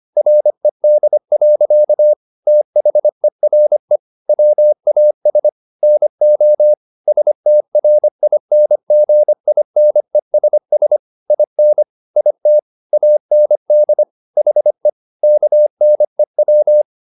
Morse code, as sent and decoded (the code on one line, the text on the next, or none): .-. -.. .-.-.- / - .... .-. / .-- .- ... / -. --- / ... - .-. .. -. --. .. -. ... ... / .. -. / .. - / .- -. -.. / .... / -.- -. .--
RED. THERE WAS NO STRINGINESS IN IT AND HE KNEW